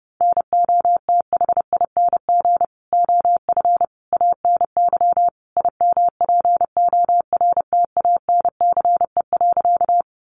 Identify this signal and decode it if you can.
Morse code, heard as a keyed tone, about 30 wpm, 700 Hz. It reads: NOTHING OF ANY IMPORTANCE.
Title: Sentences from Top 1200 Words 30wpm